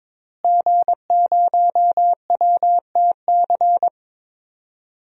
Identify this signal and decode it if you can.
Morse code, heard as a keyed tone, about 22 wpm, 700 Hz. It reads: G0WTC
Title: DXCC-100 Call Signs - 5 Characters - Repeated 2x - 22wpm